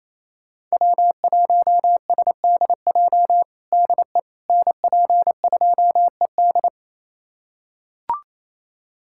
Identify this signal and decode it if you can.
Morse code, heard as a keyed tone, about 27 wpm, 700 Hz. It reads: W1SDJ DE NP2ED
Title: USA Callsign DE Callsign - 5 Characters - 28wpm